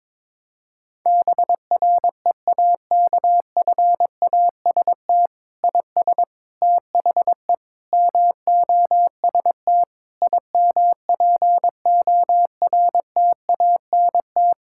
Morse code, heard as a keyed tone, about 22 wpm, 700 Hz. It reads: BREAKFAST IS THE MOST IMPORTANT
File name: Sets of 9 Words - 22wpm